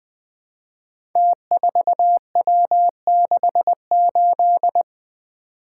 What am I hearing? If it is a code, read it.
Morse code, heard as a keyed tone, about 20 wpm, 700 Hz. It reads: T4W68
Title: Code Groups - 5 Letter:Numbers - 20wpm